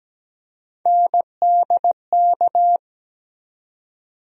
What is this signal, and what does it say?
Morse code, heard as a keyed tone, about 17 wpm, 700 Hz. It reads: NDK
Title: USA Call Sign Postfixes - 3 Letters 17wpm